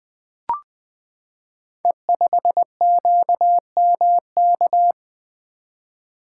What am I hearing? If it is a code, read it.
Morse code, heard as a keyed tone, about 20 wpm, 700 Hz. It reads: E5QMK
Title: Code Groups - 5 Letter:Numbers - 20wpm